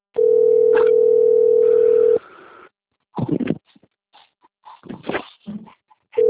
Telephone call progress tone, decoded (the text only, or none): ringback tone